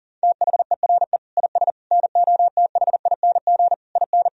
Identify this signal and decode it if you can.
Morse code, heard as a keyed tone, about 40 wpm, 700 Hz. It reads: THERE IS NOTHING IN
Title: Sentences from Top 1000 Words - Review Part 1 - 40wpm